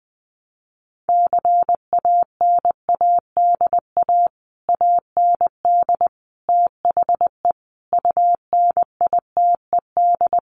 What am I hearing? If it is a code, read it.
Morse code, heard as a keyed tone, about 20 wpm, 700 Hz. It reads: CANADA AND THE UNITED
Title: Sets of 5 Words Encore 20wpm